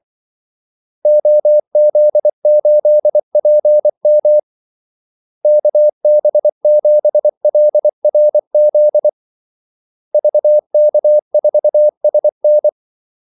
Morse code, heard as a keyed tone, about 24 wpm, 600 Hz. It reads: OZ8PM KB7LRZ VK4SN
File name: Callsigns-24wpm